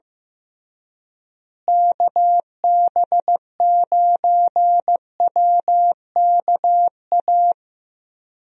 Morse code, heard as a keyed tone, about 15 wpm, 700 Hz. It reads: KB9WKA
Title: USA Call Signs - 6 Characters - Repeated 3x - 15wpm